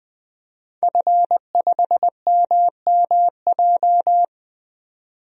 Morse code, heard as a keyed tone, about 20 wpm, 700 Hz. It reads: F5MMJ
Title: DXCC-100 Call Signs - 5 Characters - 20wpm